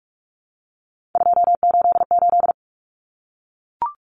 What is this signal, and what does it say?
Morse code, heard as a keyed tone, about 44 wpm, 700 Hz. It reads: 288
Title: Common Cut Numbers - 3-Digit 45wpm